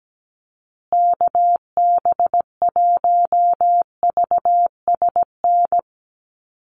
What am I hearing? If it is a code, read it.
Morse code, heard as a keyed tone, about 17 wpm, 700 Hz. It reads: KB1VSN
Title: USA Call Signs Repeated 2x 17wpm